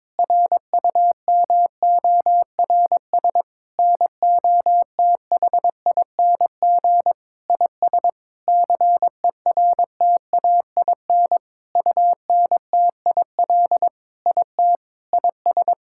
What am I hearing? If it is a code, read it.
Morse code, heard as a keyed tone, about 22 wpm, 700 Hz. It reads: RUMORS NOTHING IS CERTAIN UNTIL IT IS
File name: practice / 22wpm / Sentences with 8-Word Phrases - 22wpm